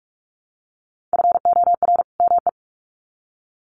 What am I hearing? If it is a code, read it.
Morse code, heard as a keyed tone, about 44 wpm, 700 Hz. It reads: FOR ME
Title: Sets of 2 Words 45wpm